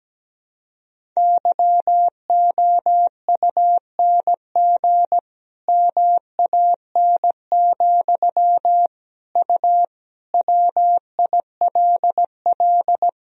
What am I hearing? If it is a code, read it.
Morse code, heard as a keyed tone, about 17 wpm, 700 Hz. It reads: YOUNG MAN, U WILL
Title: Sentences from Top 400 Words and Abbreviations 17wpm